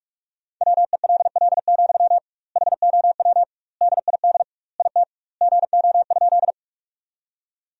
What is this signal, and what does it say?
Morse code, heard as a keyed tone, about 41 wpm, 700 Hz. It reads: WELL, HOW DID IT GO?
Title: 2x Word-Spacing - Sentences from Top 200 Words - 45wpm